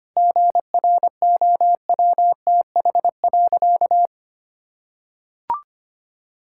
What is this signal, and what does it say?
Morse code, heard as a keyed tone, about 25 wpm, 700 Hz. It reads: GROWTH.
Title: Sentences from Top 1000 Words - Review Part 5 - 25wpm